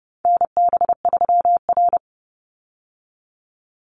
Morse code, heard as a keyed tone, about 30 wpm, 700 Hz. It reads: NB3R
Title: Call Signs Repeated 2x - 30wpm